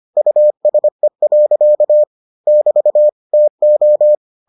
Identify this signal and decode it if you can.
Morse code, heard as a keyed tone, about 25 wpm, 600 Hz. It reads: USE. <BT> TO